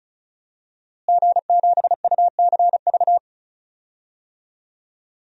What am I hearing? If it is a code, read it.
Morse code, heard as a keyed tone, about 36 wpm, 700 Hz. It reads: G7UCV
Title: DXCC-100 Call Signs - 5 Characters - Repeated 2x - 35wpm